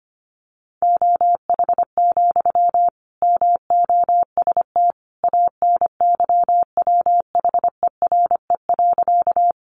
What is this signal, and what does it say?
Morse code, heard as a keyed tone, about 25 wpm, 700 Hz. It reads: OH, MOST ANYWHERE.
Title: Sentences from Top 1400 Words 25wpm